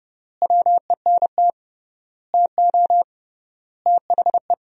Morse code, heard as a keyed tone, about 30 wpm, 700 Hz. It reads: WENT TO THE
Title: 3x Word-Spacing - Sentences from Top 400 Words - 30wpm